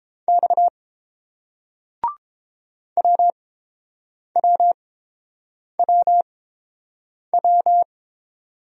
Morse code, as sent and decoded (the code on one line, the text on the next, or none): -..- / .-- .-- .-- .--
X WWWW